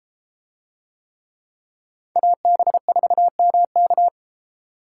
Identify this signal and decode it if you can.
Morse code, heard as a keyed tone, about 32 wpm, 700 Hz. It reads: AB4MK